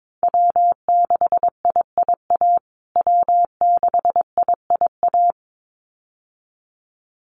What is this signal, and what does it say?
Morse code, heard as a keyed tone, about 22 wpm, 700 Hz. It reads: W6IIA W6IIA
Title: USA Call Signs Repeated 2x 22wpm